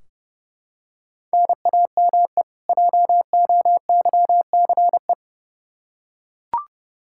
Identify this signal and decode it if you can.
Morse code, heard as a keyed tone, about 30 wpm, 700 Hz. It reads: NAME JOYCE
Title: K1USN SST QSO Practice 30wpm